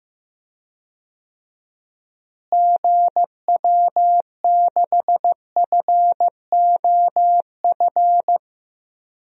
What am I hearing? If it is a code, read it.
Morse code, heard as a keyed tone, about 15 wpm, 700 Hz. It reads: GW6FOF
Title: DXCC-100 Call Signs - 6 Characters - Repeated 2x - 15wpm